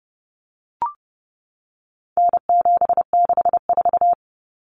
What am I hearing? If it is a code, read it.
Morse code, heard as a keyed tone, about 30 wpm, 700 Hz. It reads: N764
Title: Common Cut Numbers - 4-Digit 30wpm